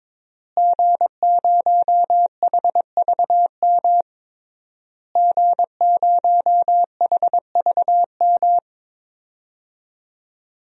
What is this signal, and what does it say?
Morse code, heard as a keyed tone, about 22 wpm, 700 Hz. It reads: G0HVM G0HVM
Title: DXCC-100 Call Signs - 5 Characters - Repeated 3x - 22wpm